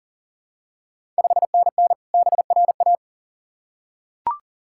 Morse code, heard as a keyed tone, about 40 wpm, 690 Hz. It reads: 5NN BRA
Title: Ontario QSO Party Exchange 40wpm